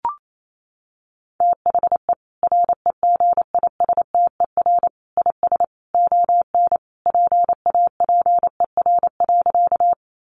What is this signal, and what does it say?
Morse code, heard as a keyed tone, about 28 wpm, 700 Hz. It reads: THE REGISTER IS ON PAPER.
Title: Sentences from Top 1200 Words 28wpm